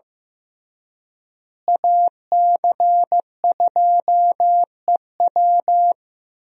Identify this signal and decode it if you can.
Morse code, heard as a keyed tone, about 16 wpm, 700 Hz. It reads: AC2EW
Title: USA Call Signs - 5 Characters - Repeated 3x - 15wpm